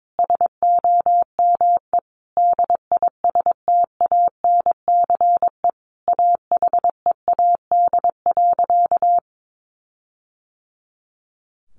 Morse code, heard as a keyed tone, about 22 wpm, 700 Hz. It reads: SOME DISTANCE AHEAD.
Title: Sentences from Top 1400 Words 22wpm